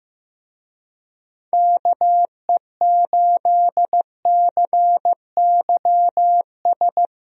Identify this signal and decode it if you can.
Morse code, heard as a keyed tone, about 15 wpm, 700 Hz. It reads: KE8CYS
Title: USA Call Signs - 6 Characters - Repeated 2x - 15wpm